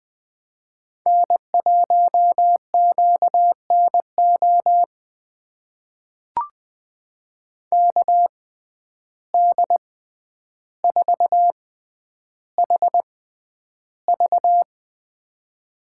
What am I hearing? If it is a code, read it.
Morse code, heard as a keyed tone, about 20 wpm, spaced as for 10 wpm, 700 Hz. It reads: N1QNO KD4HV
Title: USA Call Signs - 5 Characters - 6x Character Spacing - 20wpm